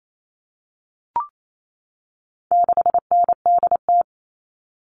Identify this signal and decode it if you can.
Morse code, heard as a keyed tone, about 28 wpm, 700 Hz. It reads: 6NDT
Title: All Cut Numbers - 4-Digit 28wpm